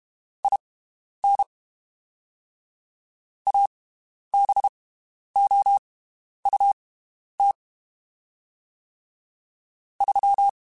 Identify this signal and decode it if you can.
Morse code, heard as a keyed tone, about 32 wpm, spaced as for 11 wpm, 800 Hz. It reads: IN ABOUT 3
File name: (spaced) / QSOs Spaced-mp3s / WD0BGP 1 @30 sp